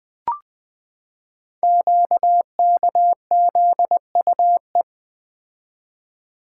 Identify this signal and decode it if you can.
Morse code, heard as a keyed tone, about 20 wpm, 700 Hz. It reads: QKZUE
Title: Code Groups - 5 Letters - 20wpm